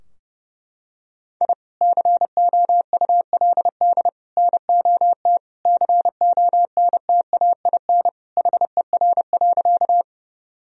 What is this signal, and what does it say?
Morse code, heard as a keyed tone, about 30 wpm, 700 Hz. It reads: I COULD NOT CONTAIN HER.
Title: Sentences from Top 1000 Words - Review Part 2 - 30wpm